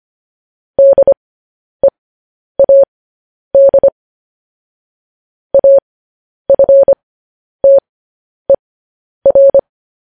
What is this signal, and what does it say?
Morse code, heard as a keyed tone, about 25 wpm, spaced as for 10 wpm, 550 Hz. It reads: DEAD AFTER
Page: news-headlines-in-morse-code-at-10-wpm